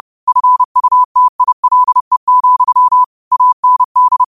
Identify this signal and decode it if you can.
Morse code, heard as a keyed tone, about 30 wpm, 1000 Hz. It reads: FUTILE, AND
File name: qotdfiles / 30wpm